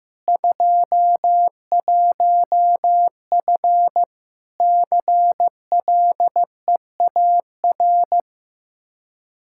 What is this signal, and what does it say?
Morse code, heard as a keyed tone, about 15 wpm, 700 Hz. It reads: 21F CLEAR
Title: Practice QSO - Exchange 2 - Info - 15wpm